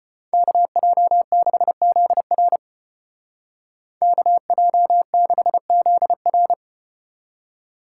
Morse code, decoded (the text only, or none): KJ6ZR KJ6ZR